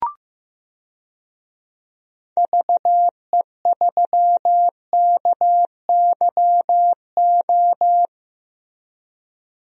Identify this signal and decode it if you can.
Morse code, heard as a keyed tone, about 15 wpm, 700 Hz. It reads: VE3KYO
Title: DXCC-100 Call Signs - 6 Characters - Repeated 2x - 15wpm